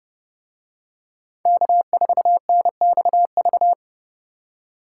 Morse code, heard as a keyed tone, about 29 wpm, 700 Hz. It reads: K4NXV